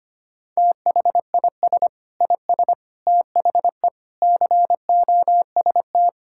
Morse code, heard as a keyed tone, about 25 wpm, 700 Hz. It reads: THIS IS THE COST
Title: Sentences from Top 1000 Words - Review Part 5 - 25wpm